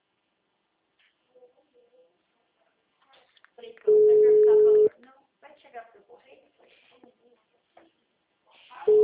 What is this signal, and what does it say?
Ringback tone (Europe, 425 Hz), with speech.